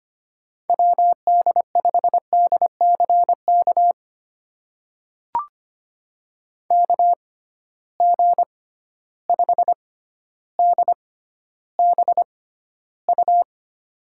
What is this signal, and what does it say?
Morse code, heard as a keyed tone, about 25 wpm, spaced as for 13 wpm, 700 Hz. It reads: WD5DCK KG5DBU
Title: USA Call Signs - 6 Characters - 6x Character Spacing - 25wpm